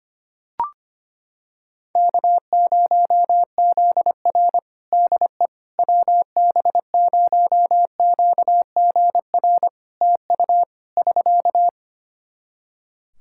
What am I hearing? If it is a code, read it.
Morse code, heard as a keyed tone, about 25 wpm, 700 Hz. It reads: K0ZR DE WB0QGR TU <SK>
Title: Practice QSO - Exchange 4 - Ending - 25wpm